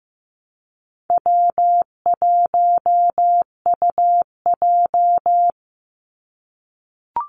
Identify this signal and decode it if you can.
Morse code, heard as a keyed tone, about 15 wpm, 700 Hz. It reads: W1UJ
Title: Easy Call Signs - 15wpm